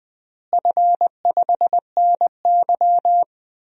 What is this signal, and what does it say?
Morse code, heard as a keyed tone, about 20 wpm, 700 Hz. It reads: F5NY
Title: DXCC-100 Call Signs - 4 Characters - Repeated 2x - 20wpm